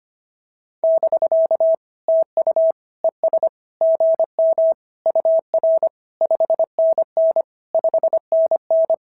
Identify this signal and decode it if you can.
Morse code, heard as a keyed tone, about 22 wpm, 650 Hz. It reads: <BK> TU ES GM UR 5NN 5NN
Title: SOTA QSO Practice 25wpm